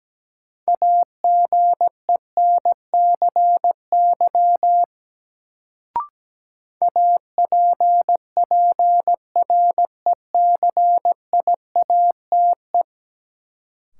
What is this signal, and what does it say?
Morse code, heard as a keyed tone, about 17 wpm, 700 Hz. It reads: AGENCY APPRECIATE
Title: Sentences from Top 1300 Words - New Word Drill - 17wpm